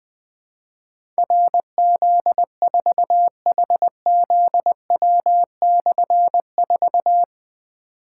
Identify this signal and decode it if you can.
Morse code, heard as a keyed tone, about 21 wpm, 700 Hz. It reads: RZ4HZW/4
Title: Difficult Call Signs 20wpm